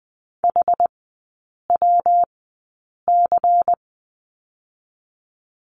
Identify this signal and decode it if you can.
Morse code, heard as a keyed tone, about 20 wpm, spaced as for 8 wpm, 700 Hz. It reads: HWC
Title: 152 Lesson 25e - Letter p - Three Characters 20wpm